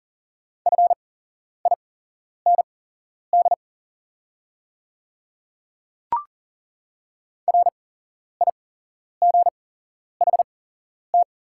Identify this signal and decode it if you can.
Morse code, heard as a keyed tone, about 40 wpm, spaced as for 11 wpm, 690 Hz. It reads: FIND RIGHT
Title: 8x Character Spacing - Top 200 Words - 40wpm